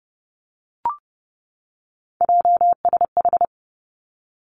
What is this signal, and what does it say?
Morse code, heard as a keyed tone, about 31 wpm, 700 Hz. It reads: JSH
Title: Three Challenging Letter-Number Combinations 30wpm